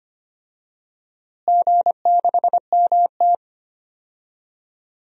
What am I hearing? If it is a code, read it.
Morse code, heard as a keyed tone, about 25 wpm, 700 Hz. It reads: G6MT
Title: DXCC-100 Call Signs - 4 Characters - Repeated 3x - 25wpm